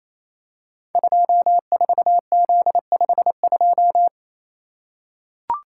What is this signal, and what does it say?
Morse code, heard as a keyed tone, about 29 wpm, 700 Hz. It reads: 24Z52
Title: Code Groups - 5 Letter:Numbers - 28wpm